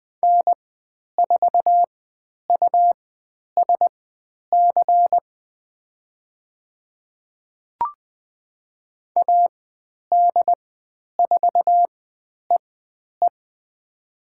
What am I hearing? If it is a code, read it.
Morse code, heard as a keyed tone, about 20 wpm, spaced as for 10 wpm, 700 Hz. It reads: N4USC AD4EE
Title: 102 Lesson 17g - Letter u - Call Signs 20wpm